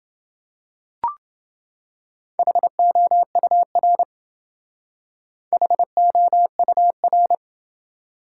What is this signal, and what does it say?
Morse code, heard as a keyed tone, about 28 wpm, 700 Hz. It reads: HOUR HOUR